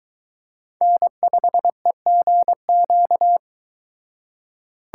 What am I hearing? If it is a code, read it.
Morse code, heard as a keyed tone, about 21 wpm, 700 Hz. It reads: N5EGQ